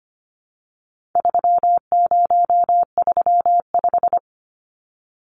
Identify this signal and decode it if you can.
Morse code, heard as a keyed tone, about 25 wpm, 700 Hz. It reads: 3035
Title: Numbers - 4-Digit 25wpm